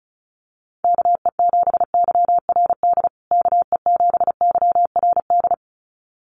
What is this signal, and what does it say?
Morse code, heard as a keyed tone, about 34 wpm, 700 Hz. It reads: KE7YRD KE7YRD
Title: USA Call Signs Repeated 2x 35wpm